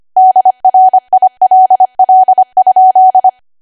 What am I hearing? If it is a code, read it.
Morse code, heard as a keyed tone, about 25 wpm, 750 Hz. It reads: DRILL?
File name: Archive / 25 WPM / 25WPM